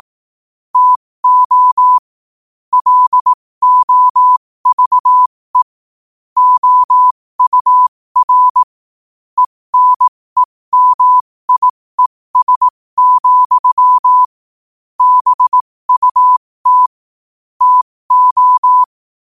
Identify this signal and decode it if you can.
Morse code, heard as a keyed tone, about 18 wpm, spaced as for 15 wpm, 1000 Hz. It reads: TO LOVE OUR ENEMIES, BUT TO
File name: qotdfiles / 15wpm